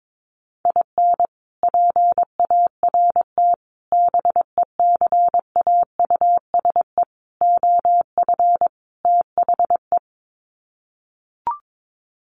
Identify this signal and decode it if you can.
Morse code, heard as a keyed tone, about 22 wpm, 700 Hz. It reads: IN PART BECAUSE OF THE
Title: Sets of 5 Words Encore2 22wpm